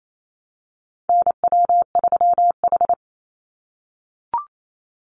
Morse code, heard as a keyed tone, about 28 wpm, 700 Hz. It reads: NW3H
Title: USA Call Signs 28wpm